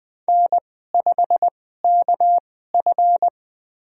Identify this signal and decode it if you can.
Morse code, heard as a keyed tone, about 20 wpm, spaced as for 15 wpm, 700 Hz. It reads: N5KF
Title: USA Call Signs - 4 Characters - 2x Character Spacing - 20wpm